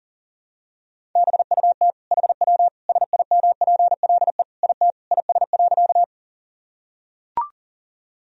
Morse code, heard as a keyed tone, about 40 wpm, 700 Hz. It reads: BUT HW SIMPLE IT IS.
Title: Sentences from Top 400 Words and Abbreviations 40wpm